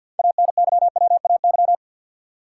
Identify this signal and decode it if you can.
Morse code, heard as a keyed tone, about 52 wpm, 700 Hz. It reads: ANYWAY
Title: Six Letter Words - 50wpm